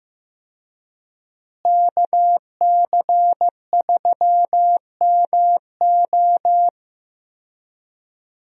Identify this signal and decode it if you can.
Morse code, heard as a keyed tone, about 15 wpm, 700 Hz. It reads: KC3MO
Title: USA Call Signs - 5 Characters - Repeated 3x - 15wpm